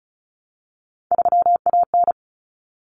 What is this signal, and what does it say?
Morse code, heard as a keyed tone, about 34 wpm, 700 Hz. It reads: 3AN
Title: Common Cut Numbers - 3-Digit 35wpm